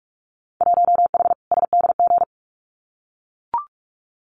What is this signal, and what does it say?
Morse code, heard as a keyed tone, about 44 wpm, 700 Hz. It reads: 1H SDG
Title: Winter Field Day Exchanges 45wpm